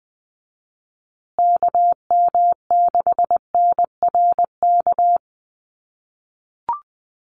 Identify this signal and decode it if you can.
Morse code, heard as a keyed tone, about 20 wpm, 700 Hz. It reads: KM6NRK
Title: USA Call Signs Repeated 2x 20wpm